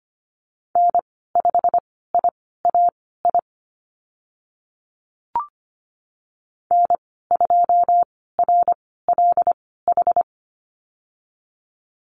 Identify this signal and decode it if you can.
Morse code, heard as a keyed tone, about 25 wpm, spaced as for 16 wpm, 700 Hz. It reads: N5IAI N2RLH
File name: Lesson 15g - Number 5 - Call Signs 25wpm